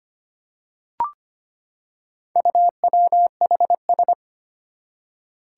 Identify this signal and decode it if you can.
Morse code, heard as a keyed tone, about 25 wpm, 700 Hz. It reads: UWHS
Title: Four Challenging Letter-Number Combinations 25wpm